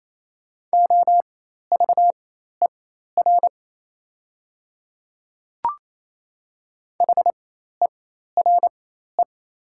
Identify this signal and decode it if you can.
Morse code, heard as a keyed tone, about 28 wpm, spaced as for 13 wpm, 700 Hz. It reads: OVER HERE
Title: 4x Character Spacing - Top 200 Words - 28wpm